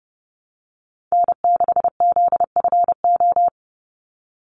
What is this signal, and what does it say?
Morse code, heard as a keyed tone, about 30 wpm, 700 Hz. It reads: N6ZFO
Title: Call Signs Repeated 2x - 30wpm